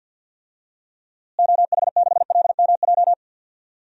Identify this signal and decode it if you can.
Morse code, heard as a keyed tone, about 52 wpm, 700 Hz. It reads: OH6LMJ
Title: DXCC-100 Call Signs - 6 Characters - Repeated 3x - 50wpm